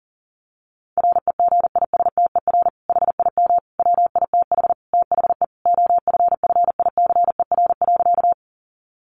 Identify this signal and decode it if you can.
Morse code, heard as a keyed tone, about 37 wpm, 700 Hz. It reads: REGISTER HIM WITH THE OFFICER.
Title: Sentences from Top 1200 Words 40wpm